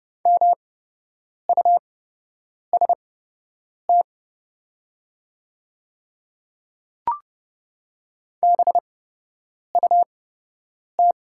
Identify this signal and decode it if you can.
Morse code, heard as a keyed tone, about 31 wpm, spaced as for 8 wpm, 700 Hz. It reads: MUST BUT
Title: 8x Character Spacing - Top 200 Words - 30wpm